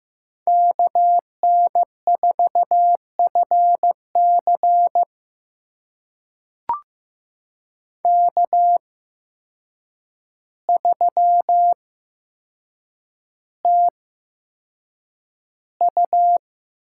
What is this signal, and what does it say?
Morse code, heard as a keyed tone, about 15 wpm, spaced as for 7 wpm, 700 Hz. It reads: KN4FC K3TU
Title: USA Call Signs - 8x Character Spacing - 15wpm